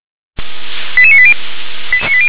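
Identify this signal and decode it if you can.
Five-tone selective call (ZVEI), heard as quick runs of five tones, tones of about 70 ms. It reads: 88980
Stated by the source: ZVEI five-tone selective call 88743 received by SDR.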